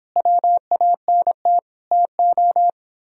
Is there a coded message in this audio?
Yes: Morse code from 0.2 to 2.7 s, heard as a keyed tone, about 29 wpm, 700 Hz. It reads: WANT TO